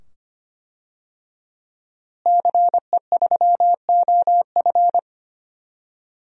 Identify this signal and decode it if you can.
Morse code, heard as a keyed tone, about 25 wpm, 700 Hz. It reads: CE3OF
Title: DXCC-100 Call Signs - 5 Characters - Repeated 2x - 25wpm